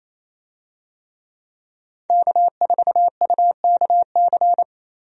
Morse code, heard as a keyed tone, about 28 wpm, 700 Hz. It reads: K4UKC
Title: USA Call Signs - 5 Characters - Repeated 3x - 28wpm